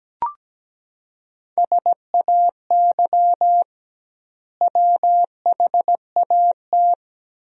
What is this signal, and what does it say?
Morse code, heard as a keyed tone, about 17 wpm, 700 Hz. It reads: SAY WHAT
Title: 2x Word-Spacing - Sentences from Top 200 Words - 17wpm